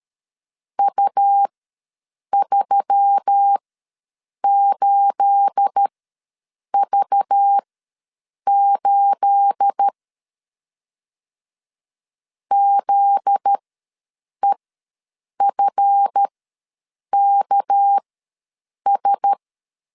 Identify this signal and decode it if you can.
Morse code, heard as a keyed tone, about 13 wpm, spaced as for 7 wpm, 800 Hz. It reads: U38V8 ZEFKS
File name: LEZIONI / 9 28 ws 12wpm 3354